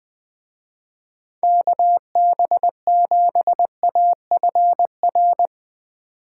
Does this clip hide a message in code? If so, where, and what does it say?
Yes: Morse code from 1.4 to 5.5 s, heard as a keyed tone, about 20 wpm, 700 Hz. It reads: KB7AFR